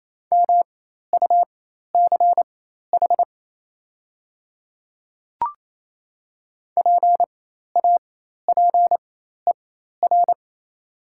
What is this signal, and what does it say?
Morse code, heard as a keyed tone, about 28 wpm, spaced as for 13 wpm, 700 Hz. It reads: MUCH PAPER
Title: 4x Character Spacing - Top 300 Words - 28wpm